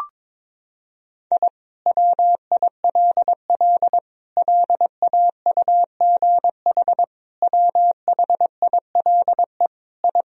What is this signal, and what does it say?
Morse code, heard as a keyed tone, about 22 wpm, 700 Hz. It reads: I WILL LAUGH WHILE I
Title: Sentences from Top 1000 Words - Review Part 1 - 22wpm